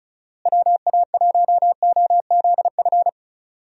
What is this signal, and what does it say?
Morse code, heard as a keyed tone, about 36 wpm, 700 Hz. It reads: WA1OZF
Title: USA Call Signs - 6 Characters - Repeated 3x - 35wpm